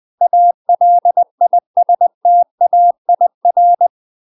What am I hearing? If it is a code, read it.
Morse code, heard as a keyed tone, about 20 wpm, 700 Hz. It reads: ALISTAIR
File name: UK MASTER QSOs TRAINING / 20WPM / UK MASTER QSOs 0041